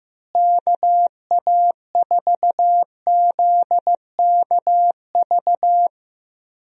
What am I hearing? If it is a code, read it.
Morse code, heard as a keyed tone, about 16 wpm, 700 Hz. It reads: KA4ZKV
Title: USA Call Signs - 6 Characters - Repeated 3x - 15wpm